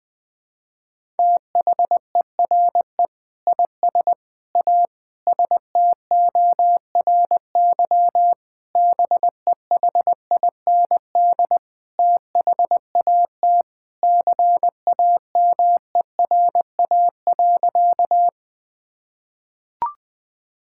Morse code, heard as a keyed tone, about 21 wpm, 700 Hz. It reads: THERE IS A STORY BEHIND THAT CAMERA.
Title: Sentences from Top 1000 Words - Review Part 5 - 20wpm